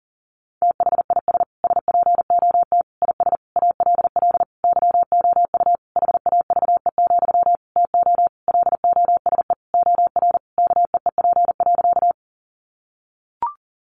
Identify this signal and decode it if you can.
Morse code, heard as a keyed tone, about 40 wpm, 700 Hz. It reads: THIS SPOT IS ALL YOU HAVE, TO LOSE OR KEEP.
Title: Sentences from Top 1100 Words 40wpm